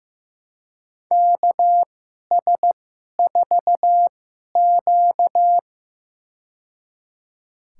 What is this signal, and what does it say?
Morse code, heard as a keyed tone, about 15 wpm, spaced as for 11 wpm, 700 Hz. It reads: KS4Q
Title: USA Call Signs - 4 Characters - 2x Character Spacing - 15wpm